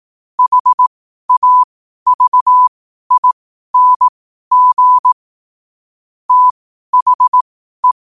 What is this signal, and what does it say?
Morse code, heard as a keyed tone, about 18 wpm, spaced as for 12 wpm, 1000 Hz. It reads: HAVING THE
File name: qotdfiles / 12wpm